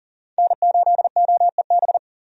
Morse code, heard as a keyed tone, about 40 wpm, 700 Hz. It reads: N8OEB
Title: USA Call Signs - 5 Characters - 40wpm